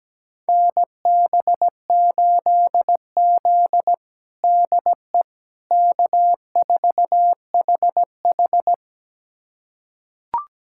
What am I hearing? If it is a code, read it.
Morse code, heard as a keyed tone, about 17 wpm, 700 Hz. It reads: NB8Z DE K4HH
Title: USA Callsign DE Callsign - 4 Characters - 17wpm